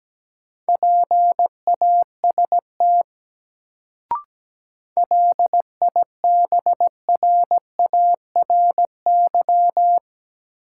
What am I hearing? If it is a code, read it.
Morse code, heard as a keyed tone, about 17 wpm, 700 Hz. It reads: PAST LIBRARY
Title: Sentences from Top 600 Words - New Word Drill - 17wpm